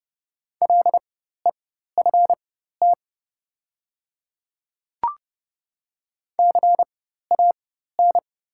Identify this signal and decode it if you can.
Morse code, heard as a keyed tone, about 30 wpm, spaced as for 14 wpm, 700 Hz. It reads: LEFT CAN
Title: 4x Character Spacing - Top 500 Words - 30wpm